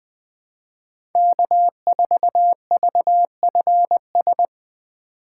Morse code, heard as a keyed tone, about 20 wpm, 700 Hz. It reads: K4VFS
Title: USA Call Signs - 5 Characters - 2x Character Spacing - 20wpm